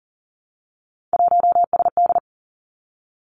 Morse code, heard as a keyed tone, about 40 wpm, 700 Hz. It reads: 1SD
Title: Three Challenging Letter-Number Combinations 40wpm